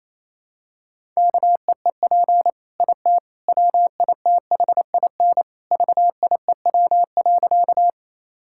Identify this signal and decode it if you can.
Morse code, heard as a keyed tone, about 28 wpm, 700 Hz. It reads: KEEP IT WITHIN VIEW.
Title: Sentences from Top 1000 Words - Review Part 6 - 28wpm